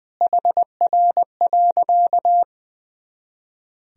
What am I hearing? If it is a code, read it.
Morse code, heard as a keyed tone, about 21 wpm, 700 Hz. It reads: HR.
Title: Sentences from Top 500 Words and Abbreviations 20wpm